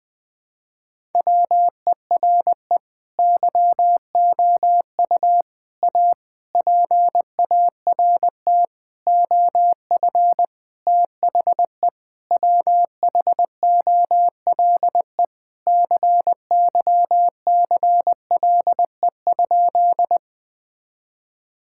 Morse code, heard as a keyed tone, about 20 wpm, 700 Hz. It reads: WERE YOU A PART OF THE WHOLE CYCLE?
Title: Sentences from Top 1000 Words - Review Part 6 - 20wpm